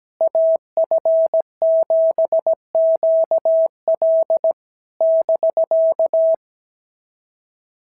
Morse code, heard as a keyed tone, about 17 wpm, 650 Hz. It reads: AF7QL <BK>
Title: SOTA QSO Practice 17wpm